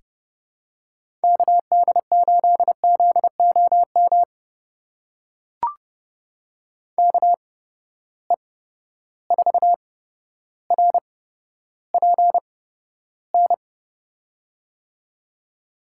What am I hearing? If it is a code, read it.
Morse code, heard as a keyed tone, about 30 wpm, spaced as for 13 wpm, 700 Hz. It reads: KD8ZOM KE4RPN
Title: USA Call Signs - 6 Characters - 8x Character Spacing - 30wpm